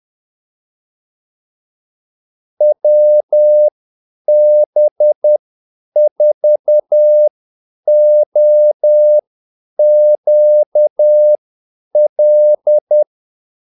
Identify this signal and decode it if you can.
Morse code, heard as a keyed tone, about 10 wpm, spaced as for 8 wpm, 600 Hz. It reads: WB4OQL
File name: Callsigns-10-8wpm